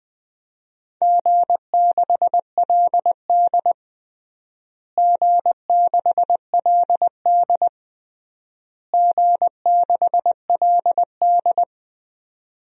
Morse code, heard as a keyed tone, about 20 wpm, 700 Hz. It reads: G6LD G6LD G6LD
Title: DXCC-100 Call Signs - 4 Characters - Repeated 3x - 20wpm